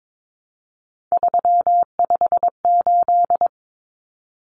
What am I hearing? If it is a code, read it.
Morse code, heard as a keyed tone, about 22 wpm, 700 Hz. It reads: 358
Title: Common Cut Numbers - 3-Digit 22wpm